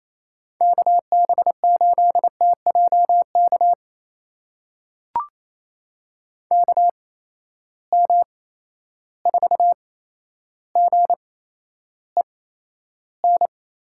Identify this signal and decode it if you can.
Morse code, heard as a keyed tone, about 28 wpm, spaced as for 12 wpm, 700 Hz. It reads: KB8TJK KM4GEN
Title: USA Call Signs - 6 Characters - 8x Character Spacing - 28wpm